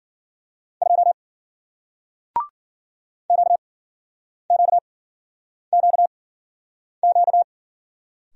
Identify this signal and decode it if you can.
Morse code, heard as a keyed tone, about 54 wpm, spaced as for 9 wpm, 700 Hz. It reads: 2 QQQQ